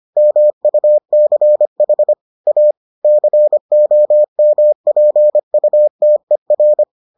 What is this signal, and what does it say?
Morse code, heard as a keyed tone, about 25 wpm, 600 Hz. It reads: MUCH A COMPUTER